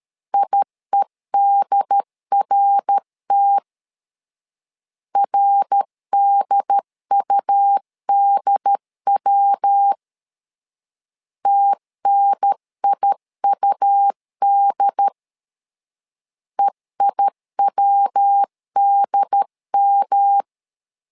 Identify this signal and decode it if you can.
Morse code, heard as a keyed tone, about 13 wpm, 800 Hz. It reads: IEDRT RDUDW TNIUD EIWDM
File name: LEZIONI / 3 MWRO / 3 MWR 12wpm 1711